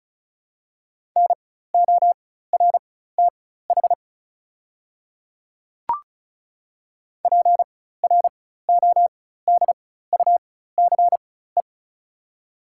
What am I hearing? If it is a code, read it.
Morse code, heard as a keyed tone, about 36 wpm, spaced as for 16 wpm, 700 Hz. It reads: NORTH PRODUCE
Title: 4x Character Spacing - Top 500 Words - 35wpm